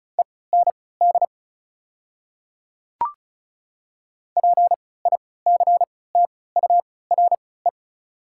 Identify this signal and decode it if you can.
Morse code, heard as a keyed tone, about 36 wpm, spaced as for 20 wpm, 700 Hz. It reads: END PICTURE
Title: 3x Character Spacing - Top 500 Words - 35wpm